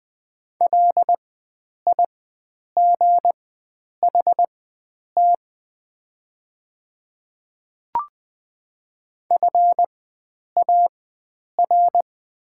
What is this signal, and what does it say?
Morse code, heard as a keyed tone, about 20 wpm, spaced as for 9 wpm, 700 Hz. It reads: LIGHT FAR
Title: 4x Character Spacing - Top 500 Words - 20wpm